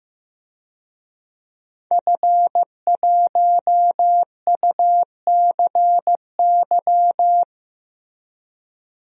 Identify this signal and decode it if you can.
Morse code, heard as a keyed tone, about 15 wpm, 700 Hz. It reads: F1UCY
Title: DXCC-100 Call Signs - 5 Characters - Repeated 3x - 15wpm